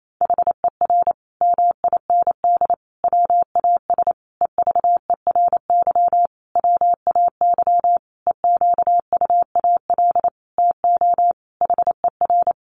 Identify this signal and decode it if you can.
Morse code, heard as a keyed tone, about 28 wpm, 700 Hz. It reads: HER MIND WAS EVERY WAY EQUAL TO HER
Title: Sentences from Top 1400 Words 28wpm